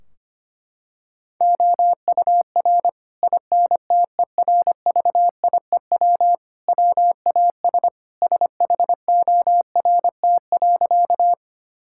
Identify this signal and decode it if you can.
Morse code, heard as a keyed tone, about 25 wpm, 700 Hz. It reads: OUR INTERVIEW WAS SHORT.
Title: Sentences from Top 1600 Words 25wpm